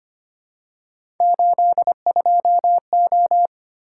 Morse code, heard as a keyed tone, about 25 wpm, 700 Hz. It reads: 82O
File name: Three Letter-Number Combinations - 25wpm